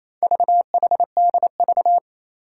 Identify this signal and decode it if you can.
Morse code, heard as a keyed tone, about 29 wpm, 700 Hz. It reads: VHDV
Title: Four Challenging Letter-Number Combinations 28wpm